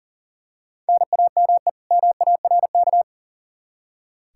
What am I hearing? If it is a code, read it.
Morse code, heard as a keyed tone, about 40 wpm, 700 Hz. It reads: NAME MARK
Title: Practice QSO - Exchange 1 - Beginning - 40wpm